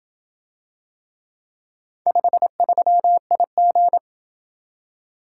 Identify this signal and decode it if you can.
Morse code, heard as a keyed tone, about 27 wpm, 700 Hz. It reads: 53IG